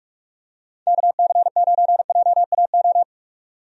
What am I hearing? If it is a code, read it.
Morse code, heard as a keyed tone, about 46 wpm, 700 Hz. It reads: KC9JAO
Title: USA Call Signs - 6 Characters - 45wpm